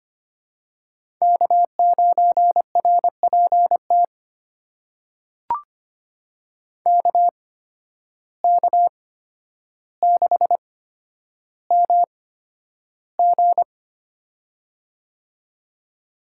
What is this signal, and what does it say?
Morse code, heard as a keyed tone, about 25 wpm, spaced as for 11 wpm, 700 Hz. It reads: K9RPT KK6MG
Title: USA Call Signs - 5 Characters - 8x Character Spacing - 25wpm